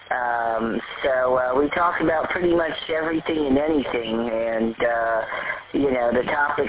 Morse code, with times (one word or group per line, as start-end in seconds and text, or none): none